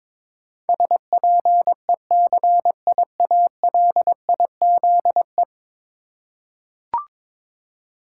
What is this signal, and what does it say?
Morse code, heard as a keyed tone, about 22 wpm, 700 Hz. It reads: SPECIALIZE
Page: morse-code-ninja-practice-22wpm